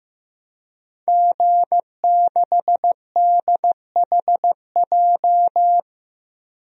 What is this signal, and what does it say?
Morse code, heard as a keyed tone, about 16 wpm, 700 Hz. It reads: G6DHJ
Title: DXCC-100 Call Signs - 5 Characters - Repeated 3x - 15wpm